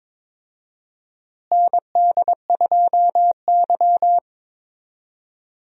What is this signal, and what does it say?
Morse code, heard as a keyed tone, about 22 wpm, 700 Hz. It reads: ND2Y
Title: USA Call Signs - 4 Characters - Repeated 2x - 22wpm